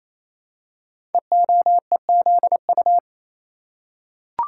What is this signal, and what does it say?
Morse code, heard as a keyed tone, about 27 wpm, 700 Hz. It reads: EOEZU
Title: Code Groups - 5 Letters - 28wpm